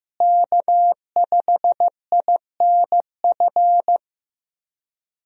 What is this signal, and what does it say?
Morse code, heard as a keyed tone, about 15 wpm, 700 Hz. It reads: K5INF
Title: USA Call Signs - 5 Characters - Repeated 3x - 15wpm